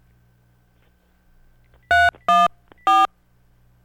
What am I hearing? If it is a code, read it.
DTMF, heard as telephone touch tones, tones of about 200 ms. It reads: 314